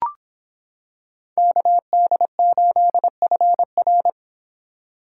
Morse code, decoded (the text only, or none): KD8FR